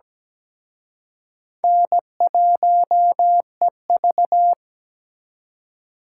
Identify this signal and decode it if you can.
Morse code, heard as a keyed tone, about 20 wpm, 700 Hz. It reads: N1EV
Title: Speed Racing - USA 1x2 Call Signs 17wpm